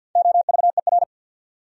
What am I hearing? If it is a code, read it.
Morse code, heard as a keyed tone, about 64 wpm, 700 Hz. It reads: OVER